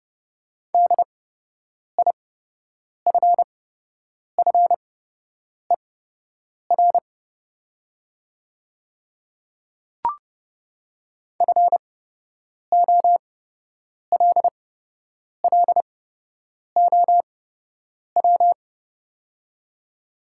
Morse code, heard as a keyed tone, about 30 wpm, spaced as for 8 wpm, 700 Hz. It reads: DIFFER FOLLOW
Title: 8x Character Spacing - Top 200 Words - 30wpm